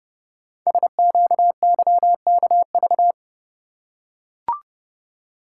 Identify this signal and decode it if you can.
Morse code, heard as a keyed tone, about 30 wpm, 700 Hz. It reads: SQYKV
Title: Code Groups - 5 Letters - 30wpm